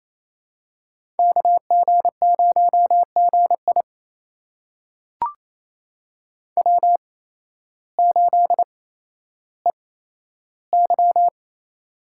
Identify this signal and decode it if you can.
Morse code, heard as a keyed tone, about 28 wpm, spaced as for 13 wpm, 700 Hz. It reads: KG0GI W8EY
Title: USA Call Signs - 8x Character Spacing - 28wpm